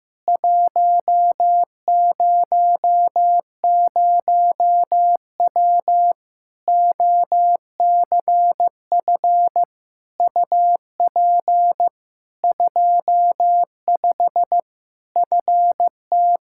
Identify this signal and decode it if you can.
Morse code, heard as a keyed tone, about 15 wpm, 700 Hz. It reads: 100W OCF UP 25 FT
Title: Practice QSO - Exchange 2 - Info - 15wpm